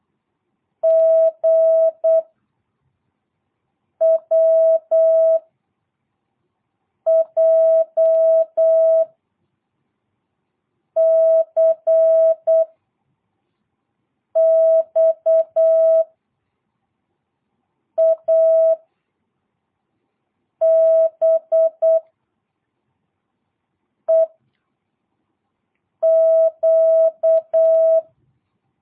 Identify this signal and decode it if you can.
Morse code, heard as a keyed tone, about 8 wpm, 660 Hz. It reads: G W J C X A B E Q
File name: jh8cbh / 2022cw